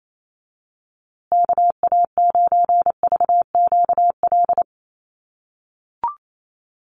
Morse code, heard as a keyed tone, about 28 wpm, 700 Hz. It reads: KA9VQL
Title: USA Call Signs Repeated 3x 28wpm